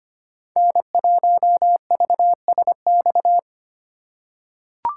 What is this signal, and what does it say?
Morse code, heard as a keyed tone, about 25 wpm, 700 Hz. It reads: N1VSX
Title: USA Call Signs - 3x Character Spacing - 25wpm